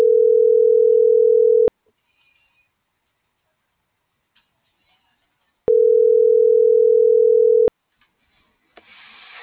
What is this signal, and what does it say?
Ringback tone (North America), 440 + 480 Hz.